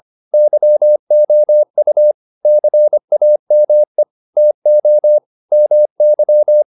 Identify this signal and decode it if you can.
Morse code, heard as a keyed tone, about 25 wpm, 600 Hz. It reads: YOU CAME TO MY